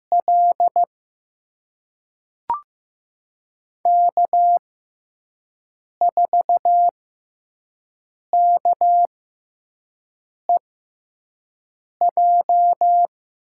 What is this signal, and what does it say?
Morse code, heard as a keyed tone, about 15 wpm, spaced as for 5 wpm, 700 Hz. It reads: L K4KEJ
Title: USA Call Signs - 6x Character Spacing - 15wpm